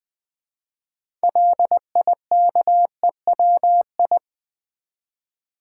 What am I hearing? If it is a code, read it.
Morse code, heard as a keyed tone, about 20 wpm, 700 Hz. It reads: LIKEWI
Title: Build A Word - 2 Letters At A Time - 8-Letter Words - 20wpm